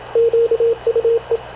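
Morse code, heard as a keyed tone, about 26 wpm, 460 Hz, with radio noise. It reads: QUE
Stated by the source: Havana Radio / CLA sending Spanish press (PX) in Morse, recorded by Radio Officer Paul Dumesnil onboard CNAV QUEST/CZDO.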